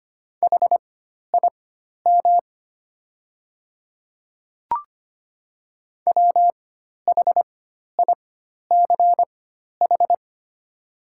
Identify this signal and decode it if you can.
Morse code, heard as a keyed tone, about 25 wpm, spaced as for 12 wpm, 700 Hz. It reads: HIM WHICH
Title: 4x Character Spacing - Top 100 Words - 25wpm